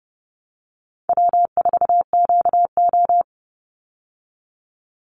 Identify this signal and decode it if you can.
Morse code, heard as a keyed tone, about 30 wpm, 700 Hz. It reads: W4QO
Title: Easy Call Signs Repeated 3x - 30wpm